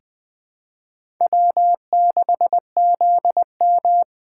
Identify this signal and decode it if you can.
Morse code, heard as a keyed tone, about 21 wpm, 700 Hz. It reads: W6ZM
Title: USA Callsign DE Callsign - 4 Characters - 20wpm